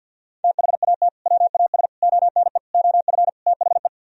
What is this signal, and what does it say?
Morse code, heard as a keyed tone, about 64 wpm, 700 Hz. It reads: THAT WAS ONE OF THE